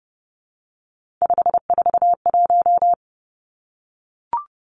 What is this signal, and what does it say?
Morse code, heard as a keyed tone, about 31 wpm, 700 Hz. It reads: 541
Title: Three Challenging Letter-Number Combinations 30wpm